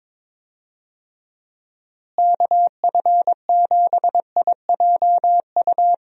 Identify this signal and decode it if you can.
Morse code, heard as a keyed tone, about 21 wpm, 700 Hz. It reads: KF7IJU
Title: USA Call Signs - 6 Characters - Repeated 3x - 22wpm